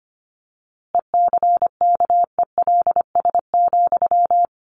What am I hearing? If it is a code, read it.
Morse code, heard as a keyed tone, about 25 wpm, 700 Hz. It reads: ECKELS,
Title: A Sound Of Thunder 25wpm